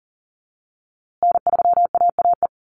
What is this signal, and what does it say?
Morse code, heard as a keyed tone, about 40 wpm, 700 Hz. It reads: N3AAE
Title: USA Call Signs Repeated 3x 40wpm